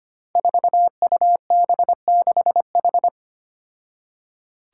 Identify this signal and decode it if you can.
Morse code, heard as a keyed tone, about 25 wpm, 700 Hz. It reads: 4UB6H
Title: Five Challenging Letter-Number Combinations 25wpm